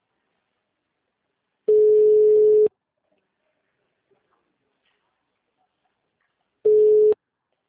Ringback tone (Europe, 425 Hz).
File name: out- -2014- - - .10